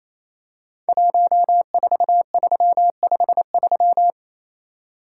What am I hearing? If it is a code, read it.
Morse code, heard as a keyed tone, about 27 wpm, 690 Hz. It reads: 14353
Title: Code Groups - 5 Letter:Numbers - 28wpm